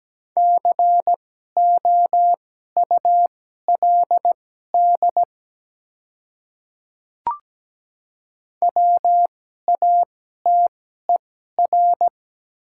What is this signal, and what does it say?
Morse code, heard as a keyed tone, about 17 wpm, spaced as for 12 wpm, 700 Hz. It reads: COULD WATER
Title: 2x Character Spacing - Top 100 Words - 17wpm